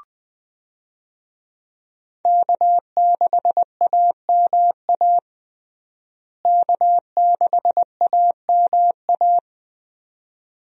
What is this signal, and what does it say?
Morse code, heard as a keyed tone, about 20 wpm, 700 Hz. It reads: K6AMA K6AMA
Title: USA Call Signs - 5 Characters - Repeated 2x - 20wpm